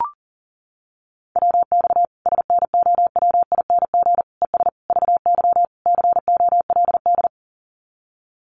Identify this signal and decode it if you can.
Morse code, heard as a keyed tone, about 40 wpm, 700 Hz. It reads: WX SNOWING ES VY COLD
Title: Common QSO Phrases 40wpm